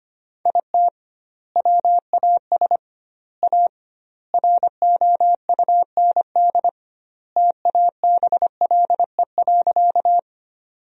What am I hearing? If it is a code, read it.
Morse code, heard as a keyed tone, about 25 wpm, 700 Hz. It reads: IT WAS A ROUND TABLE.
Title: 2x Word-Spacing - Sentences from Top 400 Words - 25wpm